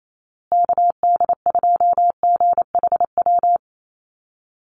Morse code, heard as a keyed tone, about 28 wpm, 700 Hz. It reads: KD2GHW
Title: USA Call Signs Repeated 3x 28wpm